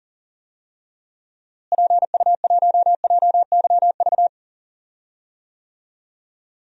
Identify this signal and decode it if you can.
Morse code, heard as a keyed tone, about 40 wpm, 700 Hz. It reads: PU1JYV
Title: DXCC-100 Call Signs - 6 Characters - Repeated 2x - 40wpm